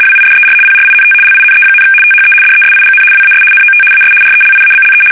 RTTY, about 75 baud. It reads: WIDER PRINT LINE, THE DESIGN WAS NOT AS RELIABLE
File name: fsk850-75B / teletype-corp